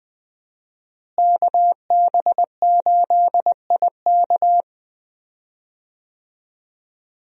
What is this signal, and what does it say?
Morse code, heard as a keyed tone, about 20 wpm, 700 Hz. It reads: KB8IK
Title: USA Call Signs - 5 Characters - Repeated 3x - 20wpm